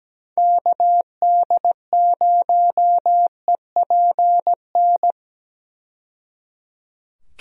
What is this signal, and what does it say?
Morse code, heard as a keyed tone, about 17 wpm, 700 Hz. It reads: KD0EPN
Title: USA Call Signs - 6 Characters - 17wpm